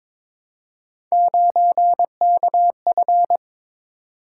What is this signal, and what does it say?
Morse code, heard as a keyed tone, about 22 wpm, 700 Hz. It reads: 9KF
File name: practice / 22wpm / Three Letter-Number Combinations - 22wpm